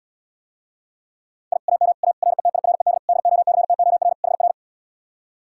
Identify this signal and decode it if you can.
Morse code, heard as a keyed tone, about 80 wpm, 700 Hz. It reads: I AM A RETIRED DOCTOR HR